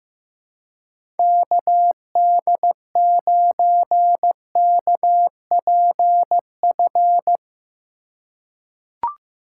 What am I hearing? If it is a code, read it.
Morse code, heard as a keyed tone, about 15 wpm, 700 Hz. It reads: KD9KPF
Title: USA Call Signs - 6 Characters - 4x Character Spacing - 15wpm